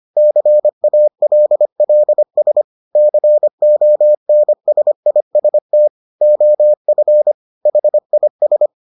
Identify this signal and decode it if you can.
Morse code, heard as a keyed tone, about 25 wpm, 600 Hz. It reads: CALLS CONSIST OF HIS